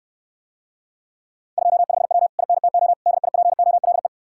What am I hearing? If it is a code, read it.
Morse code, heard as a keyed tone, about 69 wpm, 700 Hz. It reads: 25W INTO DIPOLE